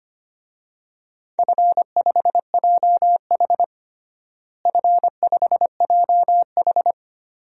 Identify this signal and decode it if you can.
Morse code, heard as a keyed tone, about 25 wpm, 700 Hz. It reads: F5JH F5JH
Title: DXCC-100 Call Signs - 4 Characters - Repeated 2x - 25wpm